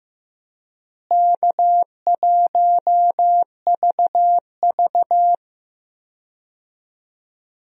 Morse code, heard as a keyed tone, about 15 wpm, 700 Hz. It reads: K1VV
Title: USA Call Signs - 4 Characters - Repeated 3x - 15wpm